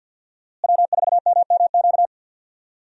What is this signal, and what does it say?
Morse code, heard as a keyed tone, about 48 wpm, 700 Hz. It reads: W4MMQ
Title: USA Call Signs - 5 Characters - Repeated 3x - 50wpm